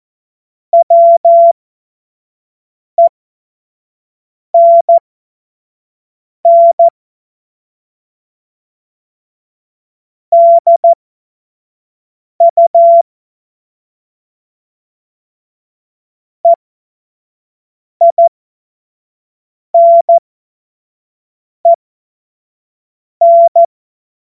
Morse code, heard as a keyed tone, about 14 wpm, spaced as for 5 wpm, 680 Hz. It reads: WENN DU EINEN
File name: Morsekurs / Witze / Witz 21. Dezember 2021, 50 F70